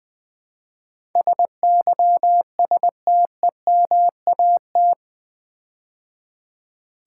Morse code, heard as a keyed tone, about 20 wpm, 700 Hz. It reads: SYSTEMAT
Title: Build A Word - 4 Letters At A Time - 10-Letter Words - 20wpm